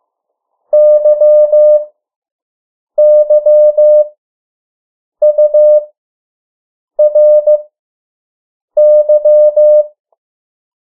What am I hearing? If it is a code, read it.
Morse code, heard as a keyed tone, about 15 wpm, spaced as for 6 wpm, 600 Hz. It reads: YYURY
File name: morse mp3 / 03 urok 2